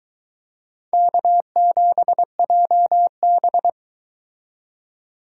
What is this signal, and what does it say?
Morse code, heard as a keyed tone, about 22 wpm, 700 Hz. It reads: K7JB